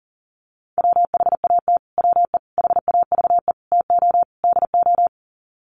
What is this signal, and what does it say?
Morse code, heard as a keyed tone, about 40 wpm, 700 Hz. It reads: WHAT WE HAVE TO DO
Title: Sets of 5 Words Encore 40wpm